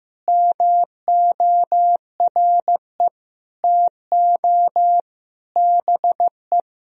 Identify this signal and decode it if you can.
Morse code, heard as a keyed tone, about 15 wpm, 700 Hz. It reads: MORE TO BE
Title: Sentences from Top 500 Words and Abbreviations 15wpm